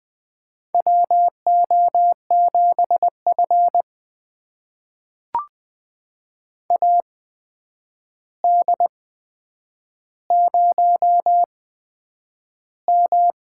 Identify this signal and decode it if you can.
Morse code, heard as a keyed tone, about 20 wpm, spaced as for 9 wpm, 700 Hz. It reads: WO7F AD0M
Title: USA Call Signs - 4 Characters - 8x Character Spacing - 20wpm